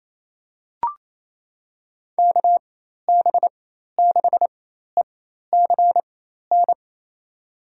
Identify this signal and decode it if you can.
Morse code, heard as a keyed tone, about 28 wpm, spaced as for 13 wpm, 700 Hz. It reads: KB6ECN
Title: USA Call Signs - 6 Characters - 4x Character Spacing - 28wpm